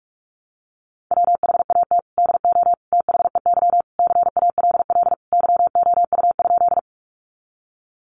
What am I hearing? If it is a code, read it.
Morse code, heard as a keyed tone, about 44 wpm, 700 Hz. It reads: WHAT DO THEY CALL YOU?
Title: Sentences from Top 100 Words 45wpm